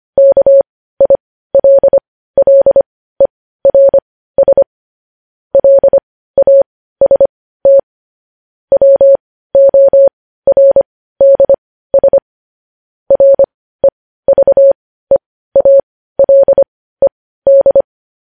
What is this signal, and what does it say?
Morse code, heard as a keyed tone, about 25 wpm, spaced as for 15 wpm, 550 Hz. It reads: KILLERS LAST WORDS REVEALED
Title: News Headlines at 15